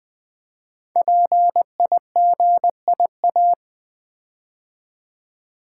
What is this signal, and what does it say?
Morse code, heard as a keyed tone, about 20 wpm, 700 Hz. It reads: PIGIA
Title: Code Groups - 5 Letters - 20wpm